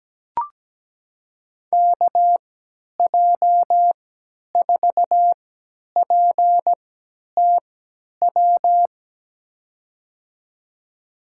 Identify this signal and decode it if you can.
Morse code, heard as a keyed tone, about 17 wpm, spaced as for 10 wpm, 700 Hz. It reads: KJ4PTW
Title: USA Call Signs - 6 Characters - 3x Character Spacing - 17wpm